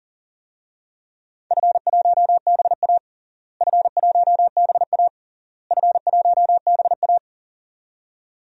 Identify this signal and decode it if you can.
Morse code, heard as a keyed tone, about 40 wpm, 700 Hz. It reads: F1BA F1BA F1BA
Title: DXCC-100 Call Signs - 4 Characters - Repeated 3x - 40wpm